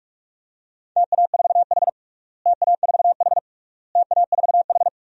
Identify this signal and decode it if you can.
Morse code, heard as a keyed tone, about 41 wpm, 700 Hz. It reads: TA4H TA4H TA4H
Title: DXCC-100 Call Signs - 4 Characters - Repeated 3x - 45wpm